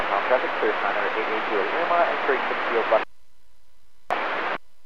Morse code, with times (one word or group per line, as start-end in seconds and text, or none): none